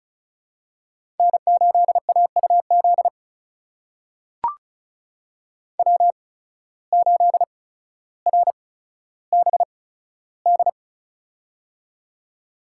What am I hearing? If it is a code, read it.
Morse code, heard as a keyed tone, about 36 wpm, spaced as for 15 wpm, 700 Hz. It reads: N8AUZ W8RBD
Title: USA Call Signs - 8x Character Spacing - 35wpm